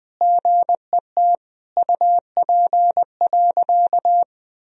Morse code, heard as a keyed tone, about 21 wpm, 700 Hz. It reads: GET UP.
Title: Sentences from Top 500 Words and Abbreviations 20wpm